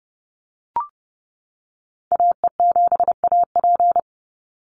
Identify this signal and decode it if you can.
Morse code, heard as a keyed tone, about 30 wpm, 700 Hz. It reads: AE7AP
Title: NA SOTA Activator Callsigns 30wpm